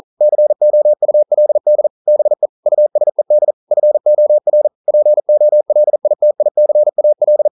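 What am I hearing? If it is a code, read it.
Morse code, heard as a keyed tone, about 42 wpm, 600 Hz. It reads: COULD BE USED FOR POLITICAL
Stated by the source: Morse code at 40 wpm, practice text, CW med Gnister.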